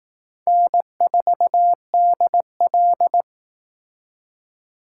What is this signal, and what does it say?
Morse code, heard as a keyed tone, about 18 wpm, 700 Hz. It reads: N4DL
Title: Speed Racing - USA 1x2 Call Signs 15wpm